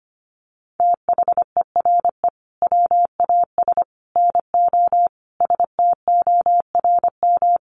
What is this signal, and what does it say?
Morse code, heard as a keyed tone, about 25 wpm, 700 Hz. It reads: THERE WAS NO STORM
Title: Sentences from Top 1500 Words 25wpm